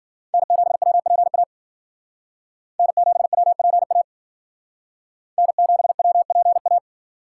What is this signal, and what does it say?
Morse code, heard as a keyed tone, about 52 wpm, 700 Hz. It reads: N7WPA N7WPA N7WPA